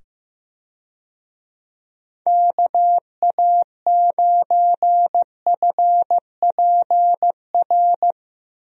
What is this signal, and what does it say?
Morse code, heard as a keyed tone, about 15 wpm, 700 Hz. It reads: KA9FPR
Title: USA Call Signs - 6 Characters - 15wpm